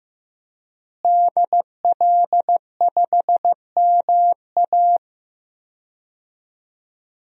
Morse code, heard as a keyed tone, about 14 wpm, 700 Hz. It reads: DL5MA
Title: DXCC-100 Call Signs - 5 Characters - Repeated 3x - 15wpm